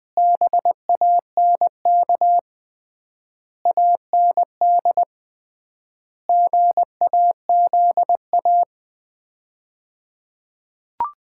Morse code, heard as a keyed tone, about 20 wpm, 700 Hz. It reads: BANK AND GAZA
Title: 3x Word-Spacing - Sets of 5 Words - 20wpm